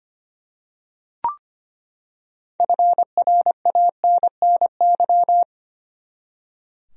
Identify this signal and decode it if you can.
Morse code, heard as a keyed tone, about 25 wpm, 700 Hz. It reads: FRANNY
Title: USA Common Names - 25wpm